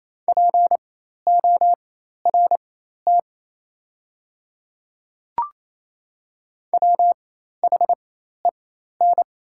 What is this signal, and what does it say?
Morse code, heard as a keyed tone, about 28 wpm, spaced as for 13 wpm, 700 Hz. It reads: PORT WHEN
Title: 4x Character Spacing - Top 200 Words - 28wpm